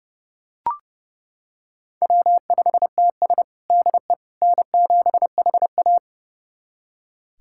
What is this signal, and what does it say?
Morse code, heard as a keyed tone, about 32 wpm, 700 Hz. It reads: W5TS DE N7HA
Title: USA Callsign DE Callsign - 4 Characters - 30wpm